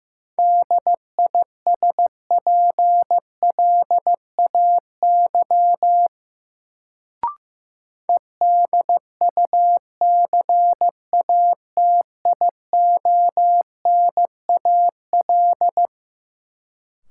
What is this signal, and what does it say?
Morse code, heard as a keyed tone, about 15 wpm, 700 Hz. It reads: DISPLAY EDUCATIONAL
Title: Sentences from Top 1200 Words - New Word Drill - 15wpm